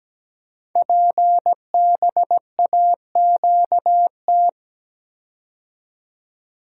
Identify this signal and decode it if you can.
Morse code, heard as a keyed tone, about 17 wpm, 700 Hz. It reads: PBAQT
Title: Code Groups - 5 Letters - 17wpm